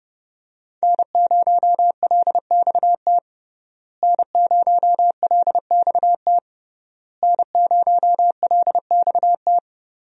Morse code, heard as a keyed tone, about 28 wpm, 700 Hz. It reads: N0LXT N0LXT N0LXT
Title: USA Call Signs - 5 Characters - Repeated 3x - 30wpm